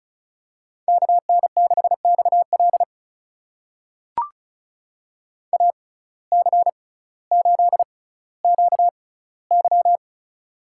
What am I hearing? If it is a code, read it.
Morse code, heard as a keyed tone, about 34 wpm, spaced as for 18 wpm, 700 Hz. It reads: KN6XL AC8QY
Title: USA Call Signs - 5 Characters - 6x Character Spacing - 35wpm